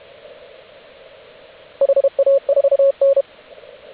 Morse code, heard as a keyed tone, about 32 wpm, 550 Hz, with radio noise. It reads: HA4N
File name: AUDIO2019 / HG8DX CW